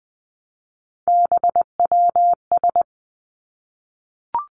Morse code, heard as a keyed tone, about 20 wpm, 700 Hz. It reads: BWS
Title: Three Challenging Letter-Number Combinations 20wpm